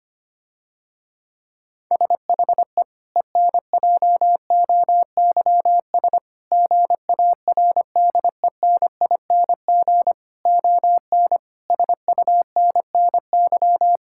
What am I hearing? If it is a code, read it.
Morse code, heard as a keyed tone, about 25 wpm, 700 Hz. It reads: SHE ENJOYS GARDENING ON SUNNY
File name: practice / 25wpm / Difficult Sentences with Hints - 25wpm